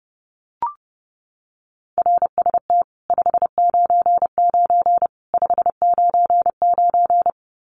Morse code, heard as a keyed tone, about 30 wpm, 700 Hz. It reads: RST 599 599
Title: Common QSO Phrases 30wpm